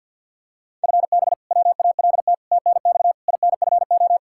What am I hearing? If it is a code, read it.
Morse code, heard as a keyed tone, about 52 wpm, 700 Hz. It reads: FB WALT TNX INFO
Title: Practice QSO - Exchange 3 - More Info - 50wpm